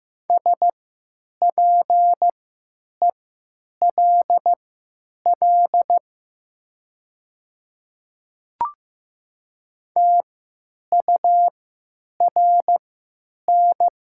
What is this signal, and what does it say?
Morse code, heard as a keyed tone, about 15 wpm, spaced as for 9 wpm, 700 Hz. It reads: SPELL TURN
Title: 3x Character Spacing - Top 300 Words - 15wpm